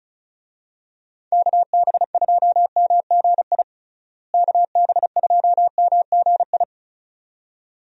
Morse code, heard as a keyed tone, about 35 wpm, 700 Hz. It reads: KB2MGI KB2MGI
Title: USA Call Signs - 6 Characters - Repeated 2x - 35wpm